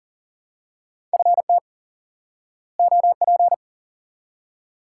Morse code, heard as a keyed tone, about 40 wpm, 700 Hz. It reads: FT OP
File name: IQE Course 3 - Lesson 89 - YAGI - Two Elements - 40wpm